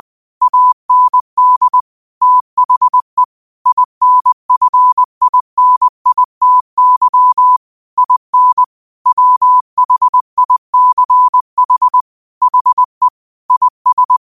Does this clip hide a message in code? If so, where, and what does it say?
Yes: Morse code from 0.4 to 14.2 s, heard as a keyed tone, about 20 wpm, 1000 Hz. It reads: AND THE INFINITY IN WHICH HE IS